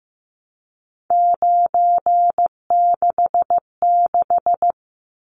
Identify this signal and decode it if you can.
Morse code, heard as a keyed tone, about 15 wpm, 700 Hz. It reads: 966
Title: Numbers - 3-Digit 15wpm